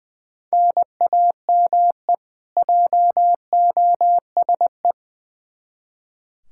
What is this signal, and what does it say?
Morse code, heard as a keyed tone, about 20 wpm, 700 Hz. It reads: NAME JOSE
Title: Practice QSO - Exchange 1 - Beginning - 20wpm